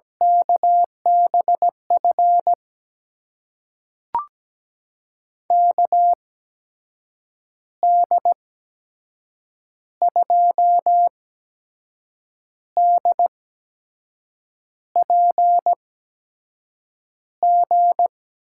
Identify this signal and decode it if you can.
Morse code, heard as a keyed tone, about 17 wpm, spaced as for 6 wpm, 700 Hz. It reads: KBF KD2DPG
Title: USA Call Signs - 8x Character Spacing - 17wpm